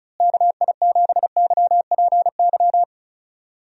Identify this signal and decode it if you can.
Morse code, heard as a keyed tone, about 34 wpm, 700 Hz. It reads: KI7YPY
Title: USA Call Signs - 6 Characters - Repeated 2x - 35wpm